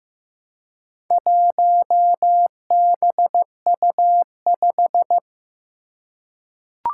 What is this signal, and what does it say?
Morse code, heard as a keyed tone, about 15 wpm, 700 Hz. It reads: 1BU5
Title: Four Challenging Letter-Number Combinations 15wpm